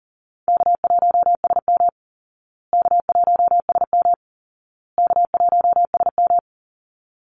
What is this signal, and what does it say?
Morse code, heard as a keyed tone, about 40 wpm, 700 Hz. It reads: K1SM K1SM K1SM
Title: Easy Call Signs Repeated 3x - 40wpm